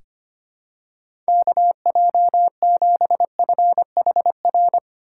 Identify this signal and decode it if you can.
Morse code, heard as a keyed tone, about 25 wpm, 700 Hz. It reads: KJ7FHR
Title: USA Call Signs - 6 Characters - 3x Character Spacing - 25wpm